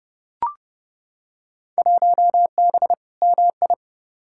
Morse code, heard as a keyed tone, about 30 wpm, 700 Hz. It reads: 1B MI
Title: Field Day Exchanges - 30wpm